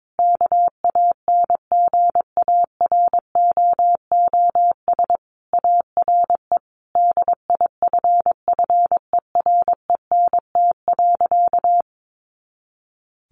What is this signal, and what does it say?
Morse code, heard as a keyed tone, about 22 wpm, 700 Hz. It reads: KANGAROOS ARE DIFFERENT.
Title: Easy Phrases 22wpm